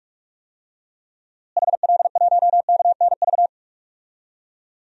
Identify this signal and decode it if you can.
Morse code, heard as a keyed tone, about 46 wpm, 700 Hz. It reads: HL1KNV
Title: DXCC-100 Call Signs - 6 Characters - Repeated 3x - 45wpm